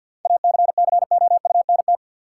Morse code, heard as a keyed tone, about 53 wpm, 700 Hz. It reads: ACCOUNT
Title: Sentences from Top 900 Words - New Word Drill - 50wpm